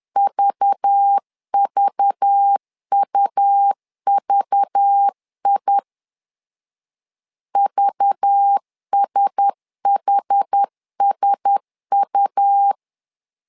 Morse code, heard as a keyed tone, about 10 wpm, 800 Hz. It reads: VVUVI VSHSU
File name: LEZIONI / 5 LBVC / 5 LBV 10wpm 2248